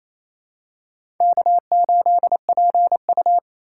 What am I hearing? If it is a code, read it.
Morse code, heard as a keyed tone, about 28 wpm, 700 Hz. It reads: K8PU
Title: USA Call Signs - 4 Characters - 4x Character Spacing - 28wpm